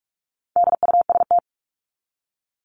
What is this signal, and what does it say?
Morse code, heard as a keyed tone, about 44 wpm, 700 Hz. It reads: DUST
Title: Top 1751-2000 Words 45wpm